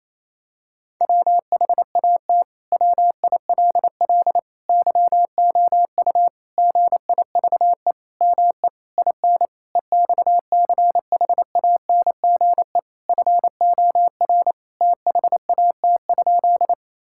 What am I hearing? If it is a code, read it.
Morse code, heard as a keyed tone, about 28 wpm, 700 Hz. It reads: WHAT WILL YOU GIVE ME IN EXCHANGE FOR THAT?
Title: Sentences from Top 1000 Words - Review Part 6 - 28wpm